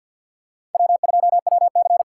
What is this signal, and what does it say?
Morse code, heard as a keyed tone, about 53 wpm, 700 Hz. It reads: W2WC
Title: USA Call Signs - 6x Character Spacing - 50wpm